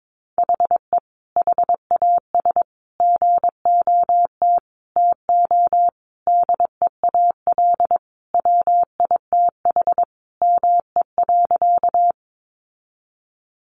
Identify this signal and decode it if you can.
Morse code, heard as a keyed tone, about 22 wpm, 700 Hz. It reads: HE HAS GOT TO DEAL WITH ME.
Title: Sentences from Top 800 Words 22wpm